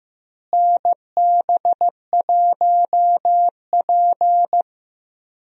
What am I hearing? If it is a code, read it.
Morse code, heard as a keyed tone, about 15 wpm, 700 Hz. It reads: NB1P
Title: USA Call Signs - 4 Characters - Repeated 3x - 15wpm